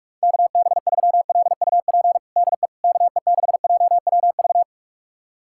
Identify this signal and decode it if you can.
Morse code, heard as a keyed tone, about 41 wpm, 700 Hz. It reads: KB3LUP DE KE6JWV
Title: USA Callsign DE Callsign 45wpm